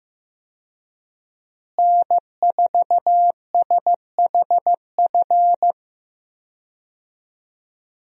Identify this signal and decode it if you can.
Morse code, heard as a keyed tone, about 15 wpm, 700 Hz. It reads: N4SHF
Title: USA Call Signs - 5 Characters - Repeated 2x - 15wpm